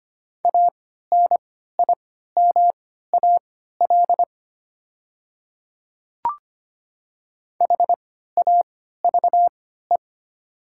Morse code, heard as a keyed tone, about 25 wpm, spaced as for 14 wpm, 700 Hz. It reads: ANIMAL HAVE
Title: 3x Character Spacing - Top 300 Words - 25wpm